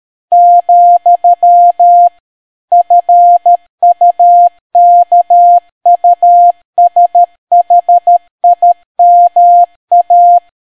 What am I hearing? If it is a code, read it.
Morse code, heard as a keyed tone, about 13 wpm, 700 Hz. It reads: , FUKUSHIMA